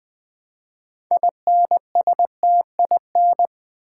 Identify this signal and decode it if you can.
Morse code, heard as a keyed tone, about 20 wpm, 700 Hz. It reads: INSTIN
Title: Build A Word - 3 Letters At A Time - 8-Letter Words - 20wpm